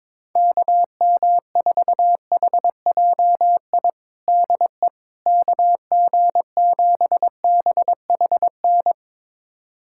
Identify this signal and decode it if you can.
Morse code, heard as a keyed tone, about 22 wpm, 700 Hz. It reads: KM4HJI DE KG7BHN
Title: USA Callsign DE Callsign - 6 Characters - 22wpm